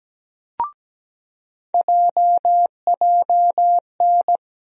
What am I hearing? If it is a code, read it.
Morse code, heard as a keyed tone, about 17 wpm, 700 Hz. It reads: JJN
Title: USA Call Sign Postfixes - 3 Letters 17wpm